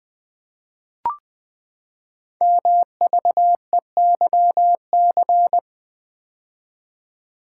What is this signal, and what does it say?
Morse code, heard as a keyed tone, about 20 wpm, 700 Hz. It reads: MVEYC
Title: Code Groups - 5 Letters - 20wpm